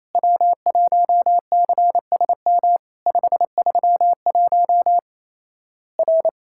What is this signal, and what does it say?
Morse code, heard as a keyed tone, about 27 wpm, 700 Hz. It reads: W1CSM 531 R
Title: Worked All Europe DX Contest - QTC Exchange - 28wpm